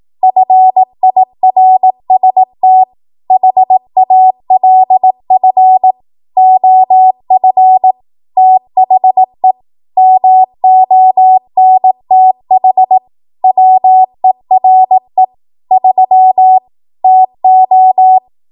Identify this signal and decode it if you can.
Morse code, heard as a keyed tone, about 18 wpm, 750 Hz. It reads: FIRST HALF OF THE MONTH WERE 3 TO